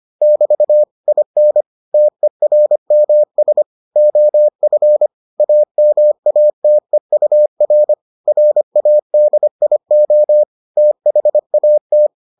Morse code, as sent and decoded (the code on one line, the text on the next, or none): -...- / .. -. / - . .-. -- ... / --- ..-. / .- -- .- - . ..- .-. / .-. .- -.. .. --- / - .... .- -
<BT> IN TERMS OF AMATEUR RADIO THAT